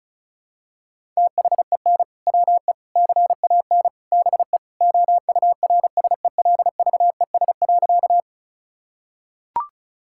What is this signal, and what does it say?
Morse code, heard as a keyed tone, about 36 wpm, 700 Hz. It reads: THEN WE CAN BE OURSELVES.
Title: Sentences from Top 1600 Words 35wpm